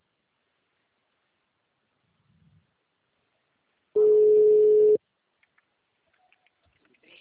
Ringback tone (Europe, 425 Hz).
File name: out- -2020- - - .49846